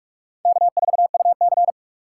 Morse code, heard as a keyed tone, about 40 wpm, 700 Hz. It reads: K4UC